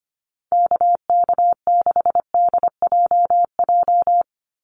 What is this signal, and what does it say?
Morse code, heard as a keyed tone, about 25 wpm, 700 Hz. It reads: KK6DJJ
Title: USA Call Signs 25wpm